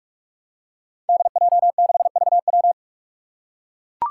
Morse code, heard as a keyed tone, about 44 wpm, 700 Hz. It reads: DJ6VW
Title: Five Challenging Letter-Number Combinations 45wpm